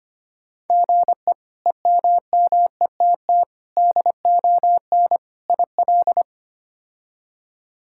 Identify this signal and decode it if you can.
Morse code, heard as a keyed tone, about 25 wpm, 700 Hz. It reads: GE EMMETT DON IL
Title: K1USN SST QSO Practice - 25wpm